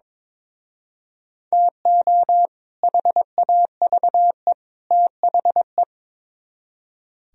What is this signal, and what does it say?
Morse code, heard as a keyed tone, about 24 wpm, 700 Hz. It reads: TO HAVE THE
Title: Speed-Racing - Sets of 3 Words 22wpm